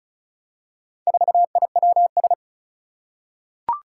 Morse code, heard as a keyed tone, about 36 wpm, 700 Hz. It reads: 4IWS
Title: Four Challenging Letter-Number Combinations 35wpm